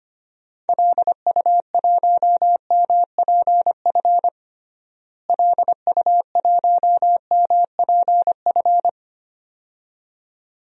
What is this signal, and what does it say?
Morse code, heard as a keyed tone, about 25 wpm, 700 Hz. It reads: LU1MPF LU1MPF
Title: DXCC-100 Call Signs - 6 Characters - Repeated 2x - 25wpm